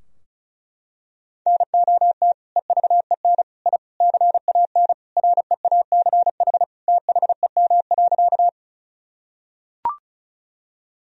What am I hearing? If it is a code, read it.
Morse code, heard as a keyed tone, about 34 wpm, 700 Hz. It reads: NOT EVEN I CAN REACH THEM.
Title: Sentences from Top 1000 Words - Review Part 2 - 35wpm